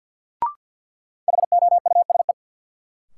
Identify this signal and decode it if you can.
Morse code, heard as a keyed tone, about 53 wpm, 700 Hz. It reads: HOUSE
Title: Sets of 2 Words - New Word Drill - 50wpm